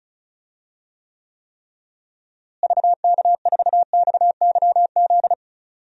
Morse code, heard as a keyed tone, about 34 wpm, 700 Hz. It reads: VK4XYZ
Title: DXCC-100 Call Signs - 6 Characters - Repeated 3x - 35wpm